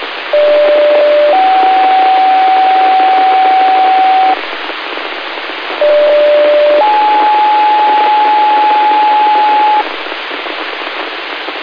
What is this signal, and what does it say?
Two-tone page, heard as a one-second tone, then a three-second tone, about 590 then 760 Hz, about 590 then 850 Hz, with noise. It is heard from 0.3 to 9.8 s.